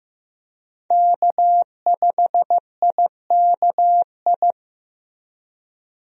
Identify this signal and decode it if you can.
Morse code, heard as a keyed tone, about 15 wpm, 700 Hz. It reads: K5IKI
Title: USA Call Signs - 5 Characters - Repeated 3x - 15wpm